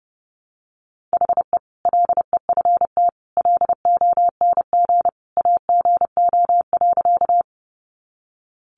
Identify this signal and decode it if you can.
Morse code, heard as a keyed tone, about 30 wpm, 700 Hz. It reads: HE LEFT LONG AGO.
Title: Sentences from Top 500 Words 30wpm